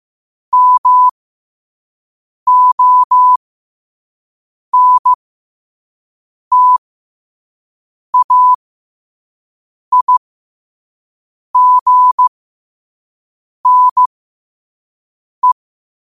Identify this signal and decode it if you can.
Morse code, heard as a keyed tone, about 15 wpm, spaced as for 5 wpm, 1000 Hz. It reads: MONTAIGNE
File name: qotdfiles / 5wpm